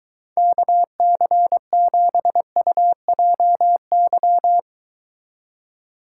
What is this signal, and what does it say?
Morse code, heard as a keyed tone, about 23 wpm, 700 Hz. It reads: KC7UJY